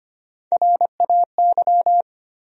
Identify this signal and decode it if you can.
Morse code, heard as a keyed tone, about 25 wpm, 700 Hz. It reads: RAY
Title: USA Common Names - Max 3 Letters - 25wpm